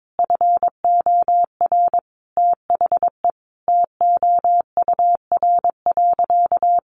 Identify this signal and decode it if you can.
Morse code, heard as a keyed tone, about 22 wpm, 700 Hz. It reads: FOR THE TOUR.
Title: Sentences from Top 1400 Words 22wpm